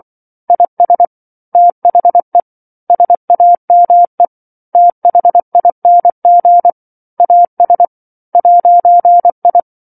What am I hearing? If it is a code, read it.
Morse code, heard as a keyed tone, about 24 wpm, 700 Hz. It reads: IS THE SAME THING AS 'I
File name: 07-Chapter-07 24 WPM